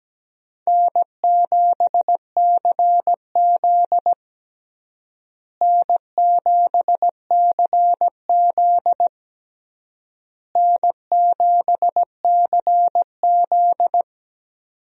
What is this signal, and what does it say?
Morse code, heard as a keyed tone, about 17 wpm, 700 Hz. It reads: N7CZ N7CZ N7CZ
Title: USA Call Signs - 4 Characters - Repeated 3x - 17wpm